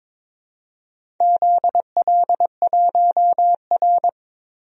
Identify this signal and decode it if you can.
Morse code, heard as a keyed tone, about 22 wpm, 700 Hz. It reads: ZL1R
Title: DXCC-100 Call Signs - 4 Characters - 22wpm